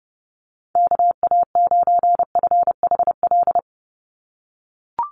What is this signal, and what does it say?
Morse code, heard as a keyed tone, about 30 wpm, 700 Hz. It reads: KA9FHL
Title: USA Call Signs 30wpm